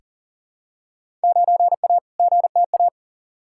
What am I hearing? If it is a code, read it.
Morse code, heard as a keyed tone, about 39 wpm, 700 Hz. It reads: 9A GTA
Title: Field Day Exchanges Encore - 40wpm